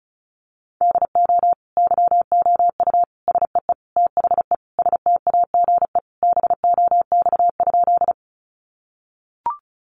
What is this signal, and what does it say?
Morse code, heard as a keyed tone, about 34 wpm, 700 Hz. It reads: DO YOU SEE THE STAGE BOX?
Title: Sentences from Top 1200 Words 35wpm